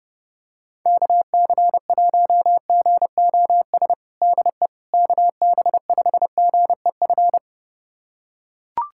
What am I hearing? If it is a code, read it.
Morse code, heard as a keyed tone, about 30 wpm, 700 Hz. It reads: KC1GOS DE KB5GEF
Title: USA Callsign DE Callsign - 6 Characters - 30wpm